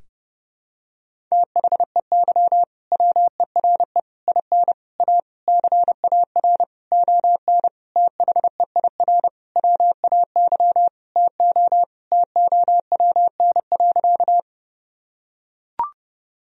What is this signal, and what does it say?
Morse code, heard as a keyed tone, about 30 wpm, 700 Hz. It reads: THEY WERE IN A CAR ON THEIR WAY TO TOWN.
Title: Sentences from Top 1000 Words - Review Part 1 - 30wpm